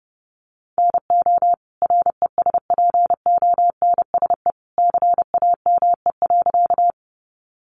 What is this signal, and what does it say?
Morse code, heard as a keyed tone, about 30 wpm, 700 Hz. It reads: NO RESPONSE CAME.
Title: Sentences from Top 1400 Words 30wpm